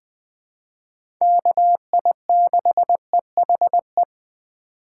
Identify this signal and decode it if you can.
Morse code, heard as a keyed tone, about 20 wpm, 700 Hz. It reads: KI6EHE
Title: USA Call Signs - 6 Characters - 2x Character Spacing - 20wpm